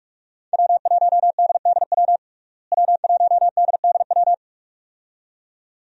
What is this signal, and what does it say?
Morse code, heard as a keyed tone, about 46 wpm, 700 Hz. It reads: W1DDW W1DDW
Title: USA Call Signs - 5 Characters - Repeated 2x - 45wpm